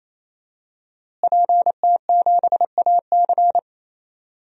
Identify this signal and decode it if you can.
Morse code, heard as a keyed tone, about 28 wpm, 700 Hz. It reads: PT7AC
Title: DXCC-100 Call Signs - 5 Characters - Repeated 3x - 28wpm